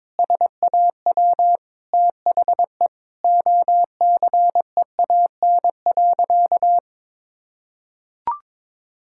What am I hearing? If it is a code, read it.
Morse code, heard as a keyed tone, about 22 wpm, 700 Hz. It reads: SAW THE OCEAN.
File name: practice / 22wpm / Sentences with 8-Word Phrases - 22wpm